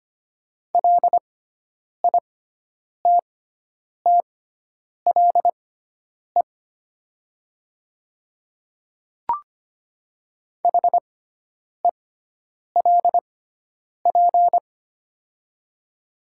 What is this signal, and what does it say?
Morse code, heard as a keyed tone, about 25 wpm, spaced as for 9 wpm, 700 Hz. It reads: LITTLE HELP
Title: 6x Character Spacing - Top 500 Words - 25wpm